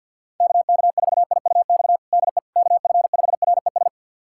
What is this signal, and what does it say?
Morse code, heard as a keyed tone, about 50 wpm, 700 Hz. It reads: KK4IUX DE KF5RES
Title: USA Callsign DE Callsign 50wpm